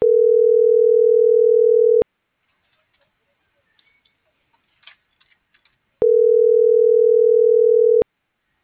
Ringback tone (North America), 440 + 480 Hz.